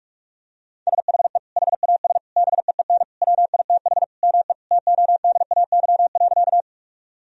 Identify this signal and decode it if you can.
Morse code, heard as a keyed tone, about 42 wpm, 700 Hz. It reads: SHE HAS BEEN WITH ME TODAY.
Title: Sentences from Top 1000 Words - Review Part 4 - 45wpm